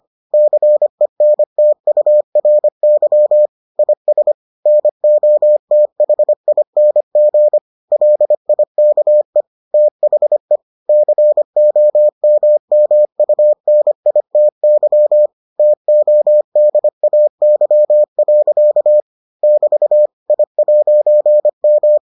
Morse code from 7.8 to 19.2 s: LIKE THE COMMUNITY TODAY.

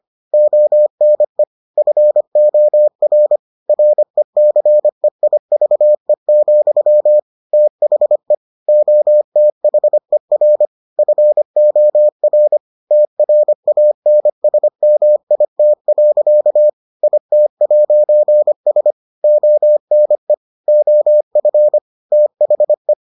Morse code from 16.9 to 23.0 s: IT'S ONE OF THE

